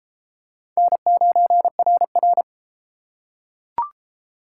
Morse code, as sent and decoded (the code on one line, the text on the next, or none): -. ----. .-. .-.
N9RR